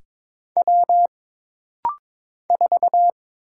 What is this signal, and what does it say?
Morse code, heard as a keyed tone, about 22 wpm, 700 Hz. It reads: W 4
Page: This rapid-fire timing is geared to push your ICR (Instant Character Recognition) skills.